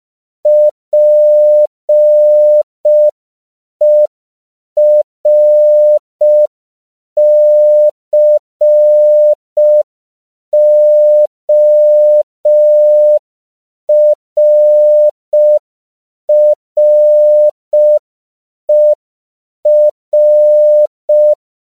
Morse code, heard as a keyed tone, about 5 wpm, 600 Hz. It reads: PERCORRER
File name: 5PPM